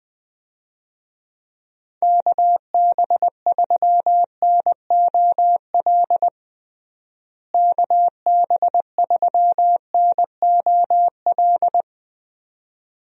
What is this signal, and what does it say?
Morse code, heard as a keyed tone, about 20 wpm, 700 Hz. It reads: KB3NOL KB3NOL
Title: USA Call Signs - 6 Characters - Repeated 2x - 20wpm